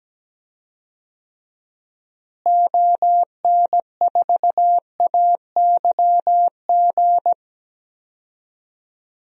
Morse code, heard as a keyed tone, about 17 wpm, 700 Hz. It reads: ON4AYG
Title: DXCC-100 Call Signs - 6 Characters - Repeated 2x - 17wpm